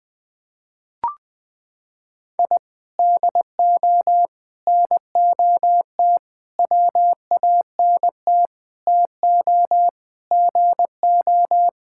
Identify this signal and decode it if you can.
Morse code, heard as a keyed tone, about 21 wpm, 700 Hz. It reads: I DO NOT WANT TO GO
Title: Sentences with 5-Word Phrases - 20wpm